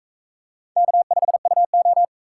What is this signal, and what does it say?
Morse code, heard as a keyed tone, about 40 wpm, 700 Hz. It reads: K5UO